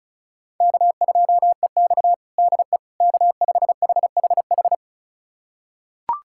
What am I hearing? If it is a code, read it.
Morse code, heard as a keyed tone, about 34 wpm, 700 Hz. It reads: K2EX DE K5HHH
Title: USA Callsign DE Callsign 35wpm